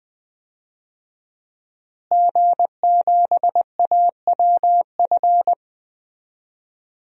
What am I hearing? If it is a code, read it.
Morse code, heard as a keyed tone, about 20 wpm, 700 Hz. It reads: G7AWF
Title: DXCC-100 Call Signs - 5 Characters - Repeated 2x - 20wpm